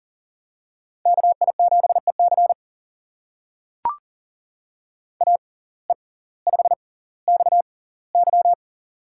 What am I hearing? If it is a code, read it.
Morse code, heard as a keyed tone, about 40 wpm, spaced as for 20 wpm, 700 Hz. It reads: KI7EC AE5XY
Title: USA Call Signs - 5 Characters - 6x Character Spacing - 40wpm